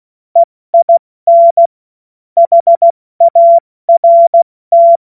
Morse code, heard as a keyed tone, about 16 wpm, 680 Hz. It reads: EIN HART
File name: Morsekurs / Witze / Witz 03. Januar 2023 70 F80